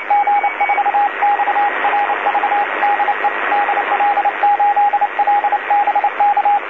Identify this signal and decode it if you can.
Morse code, heard as a keyed tone, about 28 wpm, 820 Hz, with radio noise. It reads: G4XRV DE DL8LBK
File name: DC-TRCVR / 80-CW